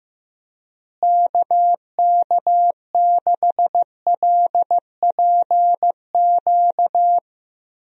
Morse code, heard as a keyed tone, about 15 wpm, 700 Hz. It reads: KK6LPQ
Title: USA Call Signs - 6 Characters - Repeated 2x - 15wpm